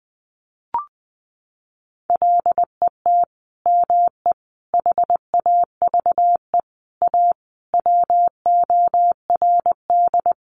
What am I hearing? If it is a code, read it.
Morse code, heard as a keyed tone, about 20 wpm, 700 Hz. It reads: LET ME HAVE A WORD
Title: Sentences from Top 900 Words 20wpm